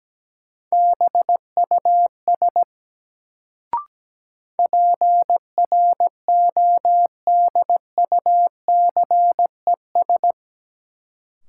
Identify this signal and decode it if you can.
Morse code, heard as a keyed tone, about 17 wpm, 700 Hz. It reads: BUS PRODUCES
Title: Sentences from Top 1000 Words - Review Part 6 - New Word Drill - 17wpm